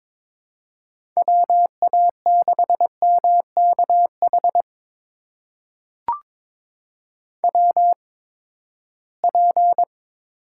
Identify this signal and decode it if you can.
Morse code, heard as a keyed tone, about 23 wpm, spaced as for 15 wpm, 700 Hz. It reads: WA6MKH WP
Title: USA Call Signs - 8x Character Spacing - 22wpm